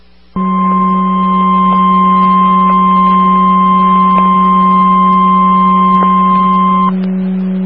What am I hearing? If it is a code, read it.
Emergency Alert System (SAME) tones (the 1050 Hz attention tone), with noise. It is heard from 0.4 to 6.9 s.
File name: W3MIE 145130 MHz Meadville ARESRACES SKYWARN Repeater